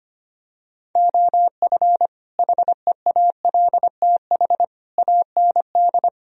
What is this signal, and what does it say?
Morse code, heard as a keyed tone, about 24 wpm, 700 Hz. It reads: OF HEALTH AND
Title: Speed-Racing - Sets of 5 Words 25wpm